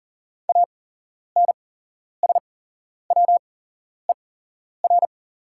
Morse code, heard as a keyed tone, about 40 wpm, spaced as for 11 wpm, 690 Hz. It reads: ANSWER
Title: 8x Character Spacing - Top 300 Words - 40wpm